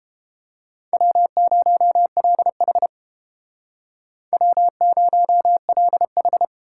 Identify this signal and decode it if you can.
Morse code, heard as a keyed tone, about 31 wpm, 700 Hz. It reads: W0LH W0LH